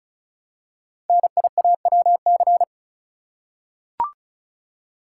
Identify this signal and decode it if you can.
Morse code, heard as a keyed tone, about 36 wpm, 690 Hz. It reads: NIAWC
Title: Code Groups - 5 Letters - 35wpm